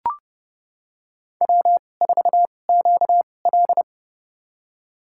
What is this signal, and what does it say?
Morse code, heard as a keyed tone, about 30 wpm, spaced as for 22 wpm, 700 Hz. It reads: W4QL
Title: USA Call Signs - 4 Characters - 2x Character Spacing - 30wpm